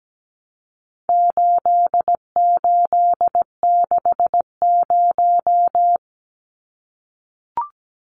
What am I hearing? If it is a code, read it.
Morse code, heard as a keyed tone, about 17 wpm, 700 Hz. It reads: 8860
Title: Numbers - 4-Digit 17wpm